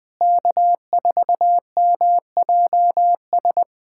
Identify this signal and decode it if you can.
Morse code, heard as a keyed tone, about 21 wpm, 700 Hz. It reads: K4MJS
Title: USA Callsign DE Callsign - 5 Characters - 20wpm